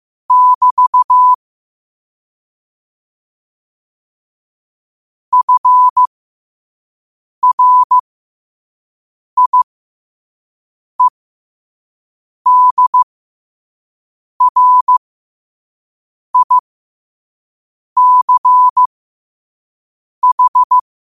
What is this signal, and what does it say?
Morse code, heard as a keyed tone, about 15 wpm, spaced as for 5 wpm, 1000 Hz. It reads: <BT> FRIEDRICH
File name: qotdfiles / 5wpm